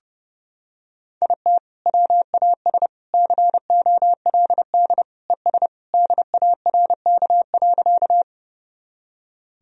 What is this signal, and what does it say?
Morse code, heard as a keyed tone, about 31 wpm, 700 Hz. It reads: IT WAS COLD ES DARK.
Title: Sentences from Top 500 Words and Abbreviations 30wpm